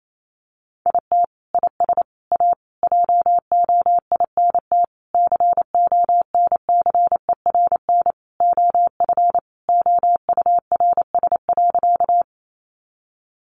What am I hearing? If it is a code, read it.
Morse code, heard as a keyed tone, about 28 wpm, 700 Hz. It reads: IT IS A JOINT CONCERN OF OURS.
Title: Sentences from Top 1300 Words 28wpm